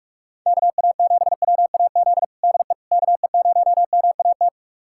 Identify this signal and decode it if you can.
Morse code, heard as a keyed tone, about 46 wpm, 700 Hz. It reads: KA7WAZ DE KE0MAT
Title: USA Callsign DE Callsign - 6 Characters - 45wpm